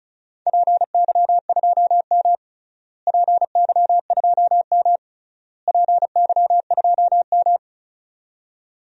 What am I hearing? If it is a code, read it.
Morse code, heard as a keyed tone, about 36 wpm, 700 Hz. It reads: PY2M PY2M PY2M
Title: DXCC-100 Call Signs - 4 Characters - Repeated 3x - 35wpm